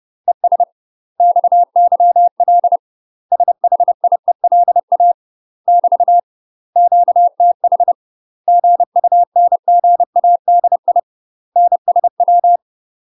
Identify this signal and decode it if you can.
Morse code, heard as a keyed tone, about 30 wpm, 700 Hz. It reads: ES XYL SHIELA <BT> QTH GUNGADI NSW
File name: UK MASTER QSOs TRAINING / 30WPM / UK MASTER QSOs 0001